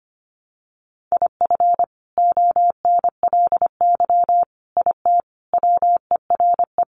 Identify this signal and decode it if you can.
Morse code, heard as a keyed tone, about 25 wpm, 700 Hz. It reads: IF ONLY IT WERE
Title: Sentences from Top 500 Words 25wpm